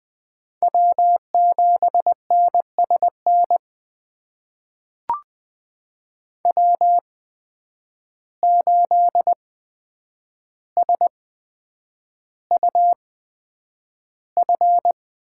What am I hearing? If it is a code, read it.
Morse code, heard as a keyed tone, about 20 wpm, spaced as for 9 wpm, 700 Hz. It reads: W7NSN W8SUF
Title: USA Call Signs - 5 Characters - 8x Character Spacing - 20wpm